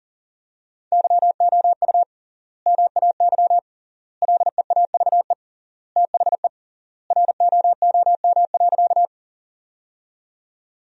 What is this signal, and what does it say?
Morse code, heard as a keyed tone, about 40 wpm, 700 Hz. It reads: YOU MAY LEAVE THE ROOM.
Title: 3x Word-Spacing - Sentences from Top 400 Words - 40wpm